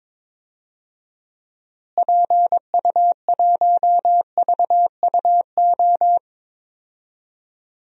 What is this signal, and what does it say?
Morse code, heard as a keyed tone, about 21 wpm, 700 Hz. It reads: PU1VUO
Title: DXCC-100 Call Signs - 6 Characters - Repeated 3x - 22wpm